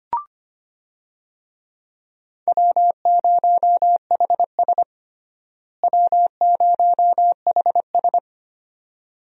Morse code, heard as a keyed tone, about 25 wpm, 700 Hz. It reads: W0HS W0HS
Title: USA Call Signs - 4 Characters - Repeated 2x - 25wpm